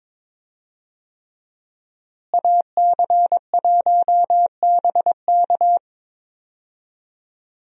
Morse code, heard as a keyed tone, about 22 wpm, 700 Hz. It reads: AC1BK
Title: USA Call Signs - 5 Characters - Repeated 3x - 22wpm